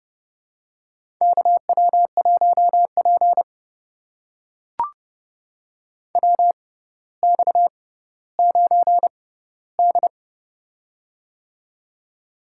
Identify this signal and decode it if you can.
Morse code, heard as a keyed tone, about 31 wpm, spaced as for 15 wpm, 700 Hz. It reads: KW1P WX9D
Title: USA Call Signs - 4 Characters - 6x Character Spacing - 30wpm